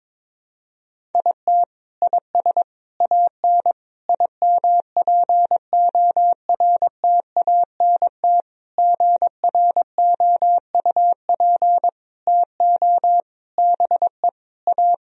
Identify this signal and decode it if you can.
Morse code, heard as a keyed tone, about 22 wpm, 700 Hz. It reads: IT IS AN IMPORTANT GROUP TO BE A
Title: Sentences from Top 1000 Words - Review Part 1 - 22wpm